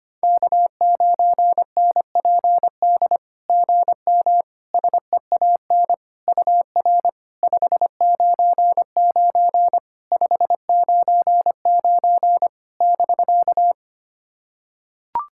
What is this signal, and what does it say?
Morse code, heard as a keyed tone, about 25 wpm, 690 Hz. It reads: K9NPD GM SEAN UR 599 599 <BK>
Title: SOTA QSO Practice 25wpm